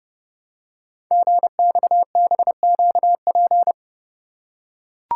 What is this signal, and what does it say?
Morse code, heard as a keyed tone, about 30 wpm, 700 Hz. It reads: GXBQP
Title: Code Groups - 5 Letters - 30wpm